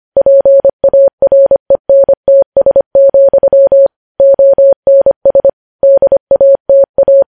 Morse code, heard as a keyed tone, about 25 wpm, 550 Hz. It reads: PARENTS, ONS DATA
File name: podcast / News headlines-1x-25x25